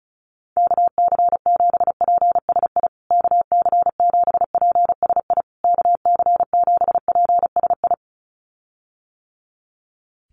Morse code, heard as a keyed tone, about 34 wpm, 700 Hz. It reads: KC7PSI KC7PSI KC7PSI
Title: USA Call Signs Repeated 3x 35wpm